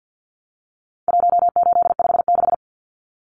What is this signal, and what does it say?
Morse code, heard as a keyed tone, about 50 wpm, 700 Hz. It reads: 1856
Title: Numbers - 4-Digit 50wpm